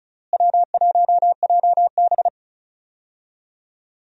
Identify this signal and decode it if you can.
Morse code, heard as a keyed tone, about 34 wpm, 700 Hz. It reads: W1JB
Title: USA Call Signs - 4 Characters - Repeated 2x - 35wpm